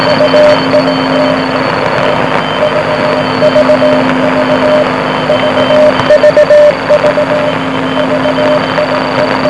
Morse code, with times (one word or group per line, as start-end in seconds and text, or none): none